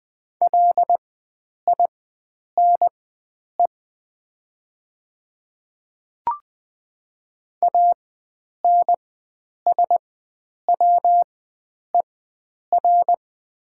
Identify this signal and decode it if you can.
Morse code, heard as a keyed tone, about 20 wpm, spaced as for 9 wpm, 700 Hz. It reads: LINE ANSWER
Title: 4x Character Spacing - Top 300 Words - 20wpm